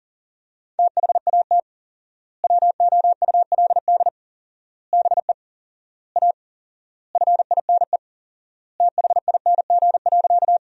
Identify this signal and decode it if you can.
Morse code, heard as a keyed tone, about 37 wpm, 700 Hz. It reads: THAT WOULD BE A FINE THING.
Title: 4x Word-Spacing - Sentences from Top 500 Words - 40wpm